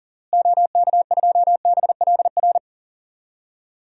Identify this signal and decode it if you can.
Morse code, heard as a keyed tone, about 40 wpm, 700 Hz. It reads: OK2BLR
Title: DXCC-100 Call Signs - 6 Characters - Repeated 3x - 40wpm